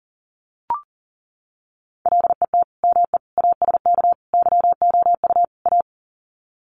Morse code, heard as a keyed tone, about 40 wpm, 700 Hz. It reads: LET ME ASK YOU A
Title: Sets of 5 Words Encore 40wpm